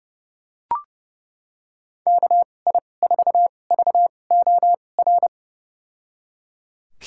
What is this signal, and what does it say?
Morse code, heard as a keyed tone, about 30 wpm, spaced as for 22 wpm, 700 Hz. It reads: KI4VOR
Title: USA Call Signs - 6 Characters - 2x Character Spacing - 30wpm